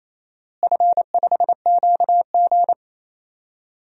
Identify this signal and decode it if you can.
Morse code, heard as a keyed tone, about 28 wpm, 700 Hz. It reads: F5QG
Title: DXCC-100 Call Signs - 4 Characters - Repeated 2x - 28wpm